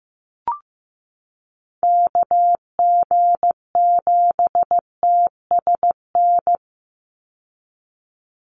Morse code, heard as a keyed tone, about 15 wpm, 700 Hz. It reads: KG7TSN
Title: USA Call Signs 15wpm